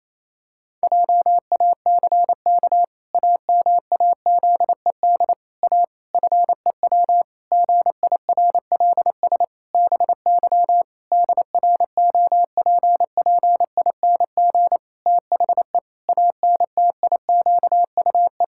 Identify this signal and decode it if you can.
Morse code, heard as a keyed tone, about 28 wpm, 700 Hz. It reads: JACK AMAZED A FEW GIRLS BY DROPPING THE ANTIQUE
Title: Pangram Challenge v2 - 28wpm